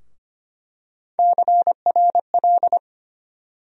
Morse code, heard as a keyed tone, about 25 wpm, 700 Hz. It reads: CRL
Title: USA Call Sign Postfixes - 3 Letters 25wpm